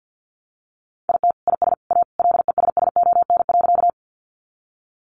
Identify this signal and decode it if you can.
Morse code, heard as a keyed tone, about 49 wpm, 700 Hz. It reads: IT IS A LESSON.
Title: Sentences from Top 1200 Words 50wpm